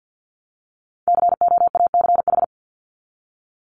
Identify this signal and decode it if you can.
Morse code, heard as a keyed tone, about 50 wpm, 700 Hz. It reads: COACH
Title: Top 1751-2000 Words 50wpm